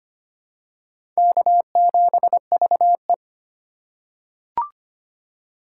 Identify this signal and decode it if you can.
Morse code, heard as a keyed tone, about 25 wpm, 700 Hz. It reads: K7VE
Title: USA Call Signs - 3x Character Spacing - 25wpm